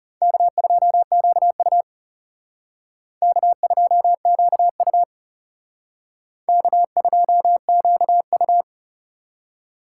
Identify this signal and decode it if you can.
Morse code, heard as a keyed tone, about 34 wpm, 700 Hz. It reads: K2QU K2QU K2QU
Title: Speed Racing - USA 1x2 Call Signs 30wpm